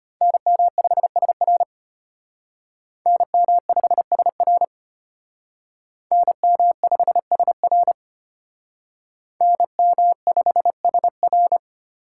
Morse code, heard as a keyed tone, about 32 wpm, 700 Hz. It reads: NM5SR NM5SR NM5SR NM5SR